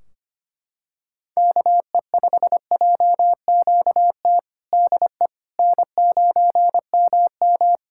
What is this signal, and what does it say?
Morse code, heard as a keyed tone, about 25 wpm, 700 Hz. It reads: KE5JQT DE N9MM
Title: K1USN SST QSO Practice 25wpm